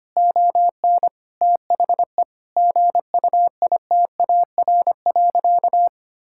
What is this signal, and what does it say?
Morse code, heard as a keyed tone, about 25 wpm, 700 Hz. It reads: ON THE GUITAR.
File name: practice / 25wpm / Difficult Sentences with Hints - 25wpm